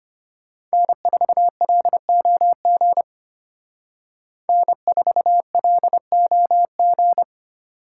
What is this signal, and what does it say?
Morse code, heard as a keyed tone, about 32 wpm, 700 Hz. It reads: N4LOG N4LOG